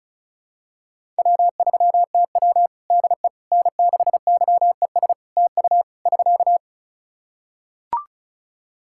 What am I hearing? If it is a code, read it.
Morse code, heard as a keyed tone, about 36 wpm, 700 Hz. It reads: W3TW DE N6YES TU <SK>
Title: Practice QSO - Exchange 4 - Ending - 35wpm